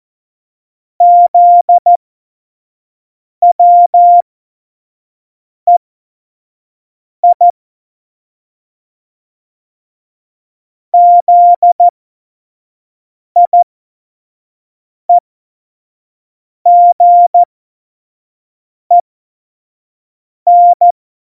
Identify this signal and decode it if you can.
Morse code, heard as a keyed tone, about 14 wpm, spaced as for 5 wpm, 700 Hz. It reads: ZWEI ZIEGEN
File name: Morsekurs / Witze / Witz 10.Juli 2018, 25 F70